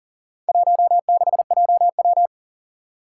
Morse code, heard as a keyed tone, about 40 wpm, 700 Hz. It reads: 16JW
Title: Four Challenging Letter-Number Combinations 40wpm